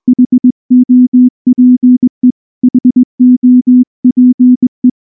Morse code, heard as a keyed tone, about 20 wpm, 250 Hz. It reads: HOPE HOPE
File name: morse-training-20WPM